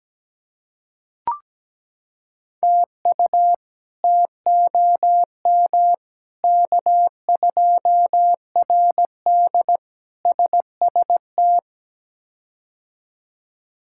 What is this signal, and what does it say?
Morse code, heard as a keyed tone, about 17 wpm, 700 Hz. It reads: TU TOM K2RD SST
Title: K1USN SST QSO Practice - 17wpm